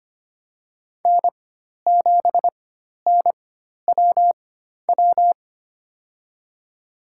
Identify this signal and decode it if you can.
Morse code, heard as a keyed tone, about 25 wpm, spaced as for 12 wpm, 700 Hz. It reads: N7NWW
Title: USA Call Signs - 5 Characters - 4x Character Spacing - 25wpm